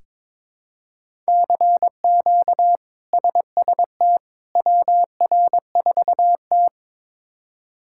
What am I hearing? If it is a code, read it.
Morse code, heard as a keyed tone, about 21 wpm, 700 Hz. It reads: CQ SST WR4T
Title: K1USN SST QSO Practice - 22wpm